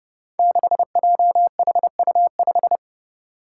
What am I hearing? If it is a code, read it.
Morse code, heard as a keyed tone, about 30 wpm, 700 Hz. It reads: 6JHU5
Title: Five Challenging Letter-Number Combinations 30wpm